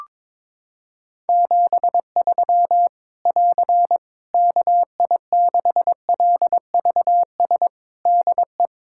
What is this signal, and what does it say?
Morse code, heard as a keyed tone, about 22 wpm, 700 Hz. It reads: 73 <AR> KI6LVS DE
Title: Practice QSO - Exchange 4 - Ending - 22wpm